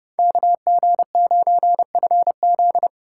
Morse code, heard as a keyed tone, about 36 wpm, 700 Hz. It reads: KG9FZ
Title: Speed Racing - USA 2x2 Call Signs 30wpm